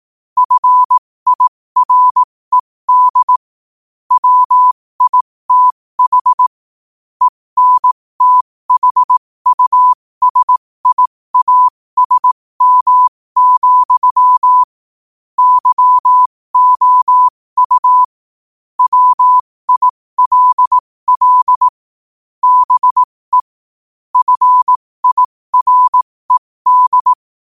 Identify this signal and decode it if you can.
Morse code, heard as a keyed tone, about 18 wpm, spaced as for 15 wpm, 1000 Hz. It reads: FIRED WITH ENTHUSIASM, YOU WILL BE FIRED
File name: qotdfiles / 15wpm